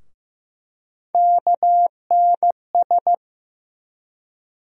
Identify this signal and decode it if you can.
Morse code, heard as a keyed tone, about 15 wpm, 700 Hz. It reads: KNS
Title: USA Call Sign Postfixes - 3 Letters 15wpm